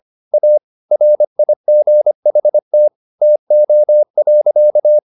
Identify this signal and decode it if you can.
Morse code, heard as a keyed tone, about 25 wpm, 600 Hz. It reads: A RIGHT TO.